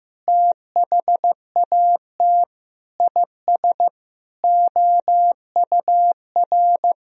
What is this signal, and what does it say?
Morse code, heard as a keyed tone, about 16 wpm, 700 Hz. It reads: THAT IS OUR
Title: Sentences from Top 1000 Words - Review Part 3 - 15wpm